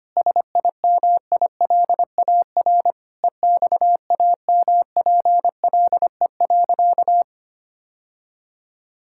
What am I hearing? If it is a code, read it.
Morse code, heard as a keyed tone, about 25 wpm, 700 Hz. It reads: SIMILAR EXAMPLE.
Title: Sentences from Top 1000 Words - Review Part 5 - 25wpm